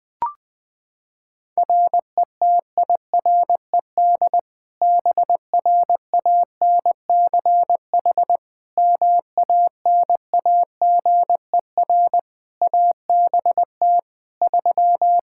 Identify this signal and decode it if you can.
Morse code, heard as a keyed tone, about 21 wpm, 700 Hz. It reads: RETIRED BRANCH MANAGER ABT 3
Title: Practice QSO - Exchange 3 - More Info - 20wpm